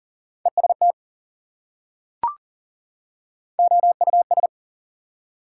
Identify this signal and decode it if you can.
Morse code, heard as a keyed tone, about 40 wpm, 700 Hz. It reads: EST OUS
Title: Prefixes and Suffixes 40wpm